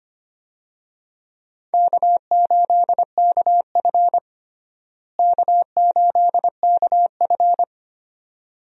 Morse code, heard as a keyed tone, about 25 wpm, 700 Hz. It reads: K8KF K8KF
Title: USA Call Signs - 4 Characters - Repeated 2x - 25wpm